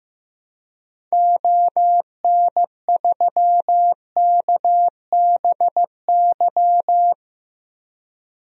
Morse code, heard as a keyed tone, about 15 wpm, 700 Hz. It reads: ON3KBY
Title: DXCC-100 Call Signs - 6 Characters - 15wpm